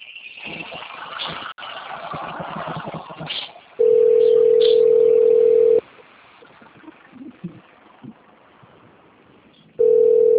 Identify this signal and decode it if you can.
Ringback tone (North America), 440 + 480 Hz.